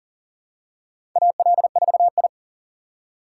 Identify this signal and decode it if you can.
Morse code, heard as a keyed tone, about 40 wpm, 700 Hz. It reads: AL4I
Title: USA Call Signs - 4 Characters - 4x Character Spacing - 40wpm